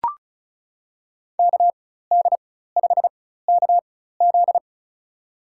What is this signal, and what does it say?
Morse code, heard as a keyed tone, about 36 wpm, spaced as for 16 wpm, 700 Hz. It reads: KD5KZ
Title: USA Call Signs - 4x Character Spacing - 35wpm